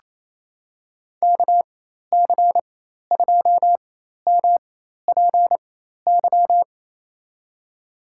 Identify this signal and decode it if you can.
Morse code, heard as a keyed tone, about 28 wpm, spaced as for 13 wpm, 700 Hz. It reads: KC2MPY
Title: USA Call Signs - 6 Characters - 4x Character Spacing - 28wpm